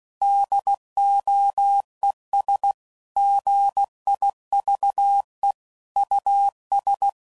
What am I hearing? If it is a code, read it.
Morse code, heard as a keyed tone, about 16 wpm, 800 Hz. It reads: DOES GIVE US
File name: CW Code Groups @15WPM / QSOs / WD0FBT